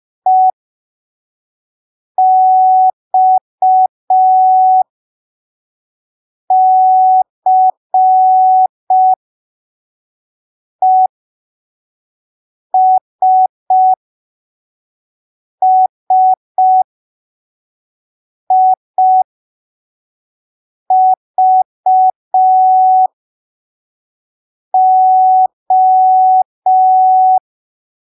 Morse code, heard as a keyed tone, about 5 wpm, spaced as for 3 wpm, 750 Hz. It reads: EXCESSIVO
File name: arquivos / texto / texto 05 5ppm